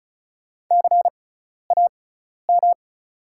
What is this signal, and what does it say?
Morse code, heard as a keyed tone, about 36 wpm, spaced as for 12 wpm, 690 Hz. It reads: CAM
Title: 6x Character Spacing - Top 200 Words - 35wpm